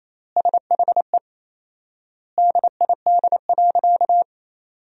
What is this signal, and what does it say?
Morse code, heard as a keyed tone, about 28 wpm, 700 Hz. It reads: SHE DID.
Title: 4x Word-Spacing - Sentences from Top 100 Words - 28wpm